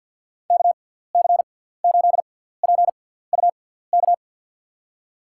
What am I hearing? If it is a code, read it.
Morse code, heard as a keyed tone, about 53 wpm, spaced as for 17 wpm, 700 Hz. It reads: KC8PUK
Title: USA Call Signs - 6 Characters - 6x Character Spacing - 50wpm